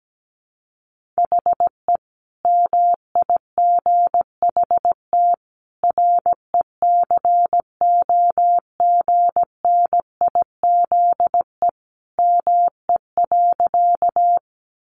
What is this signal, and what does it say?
Morse code, heard as a keyed tone, about 17 wpm, 700 Hz. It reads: HE MIGHT RECOGNIZE ME.
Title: Sentences from Top 1400 Words 17wpm